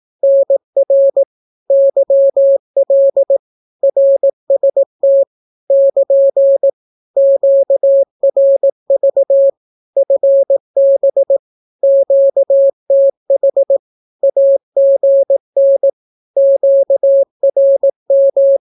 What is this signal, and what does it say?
Morse code, heard as a keyed tone, about 18 wpm, 550 Hz. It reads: NR YL RST <KN> QRV FB QTH AGN QRM
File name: naukapodstaw / skroty / skroty